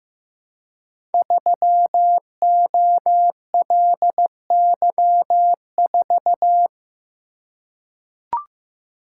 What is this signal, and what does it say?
Morse code, heard as a keyed tone, about 15 wpm, 700 Hz. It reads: 3OLY4
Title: Code Groups - 5 Letter:Numbers - 15wpm